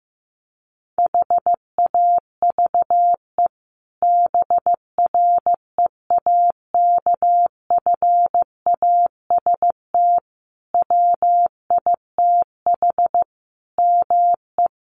Morse code, heard as a keyed tone, about 15 wpm, 700 Hz. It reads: HAVE BREAKFAST WITH ME
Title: Sentences from Top 1500 Words 15wpm